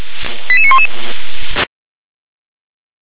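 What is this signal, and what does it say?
Five-tone selective call (ZVEI), heard as quick runs of five tones, tones of about 70 ms. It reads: 88911
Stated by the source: ZVEI five-tone selective call received on an SDR.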